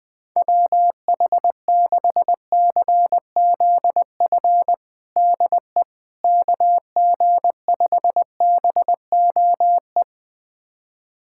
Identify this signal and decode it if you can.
Morse code, heard as a keyed tone, about 21 wpm, 700 Hz. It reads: WH6CZF DE KG5BOE
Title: USA Callsign DE Callsign - 6 Characters - 20wpm